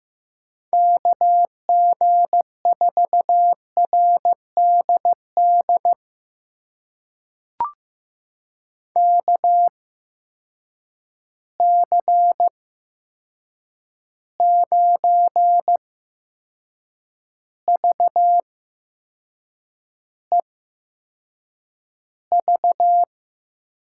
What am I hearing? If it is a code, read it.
Morse code, heard as a keyed tone, about 15 wpm, spaced as for 6 wpm, 700 Hz. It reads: KG4RDD KC9VEV
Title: USA Call Signs - 6 Characters - 8x Character Spacing - 15wpm